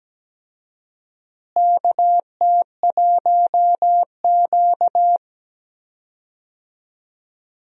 Morse code, heard as a keyed tone, about 17 wpm, 700 Hz. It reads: KT1Q
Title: USA Call Signs - 4 Characters - Repeated 3x - 17wpm